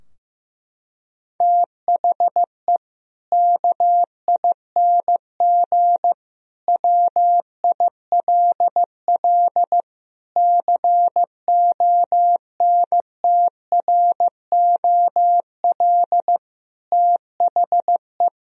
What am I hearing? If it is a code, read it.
Morse code, heard as a keyed tone, about 15 wpm, 700 Hz. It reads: THE KING WILL CONTROL THE
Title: Sentences from Top 1000 Words - Review Part 3 - 15wpm